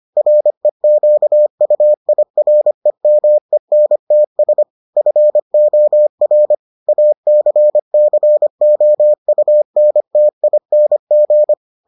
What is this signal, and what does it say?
Morse code, heard as a keyed tone, about 25 wpm, 600 Hz. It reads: REQUIREMENTS FOR ACCOUNTING